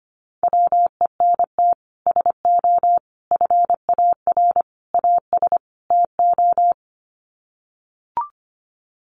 Morse code, heard as a keyed tone, about 25 wpm, 700 Hz. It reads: WENT SO FAR AS TO
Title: Sets of 5 Words Encore 25wpm